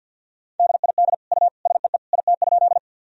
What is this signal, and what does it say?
Morse code, heard as a keyed tone, about 48 wpm, 700 Hz. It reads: DID U SEE IT?
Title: Sentences from Top 100 Words and Abbreviations 50wpm